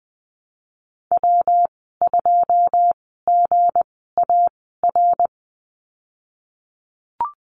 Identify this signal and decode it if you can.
Morse code, heard as a keyed tone, about 19 wpm, spaced as for 14 wpm, 700 Hz. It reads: W2GAR
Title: 167 Lesson 27g - Number 7 - Call Signs 20wpm